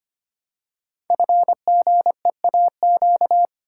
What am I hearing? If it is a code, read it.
Morse code, heard as a keyed tone, about 25 wpm, 700 Hz. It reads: FGEAQ
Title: Code Groups - 5 Letters - 25wpm